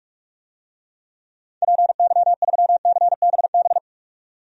Morse code, heard as a keyed tone, about 46 wpm, 700 Hz. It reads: PY3CBB
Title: DXCC-100 Call Signs - 6 Characters - 45wpm